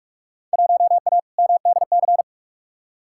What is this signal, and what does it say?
Morse code, heard as a keyed tone, about 46 wpm, 700 Hz. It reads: 1A MDC
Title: Field Day Exchanges - 45wpm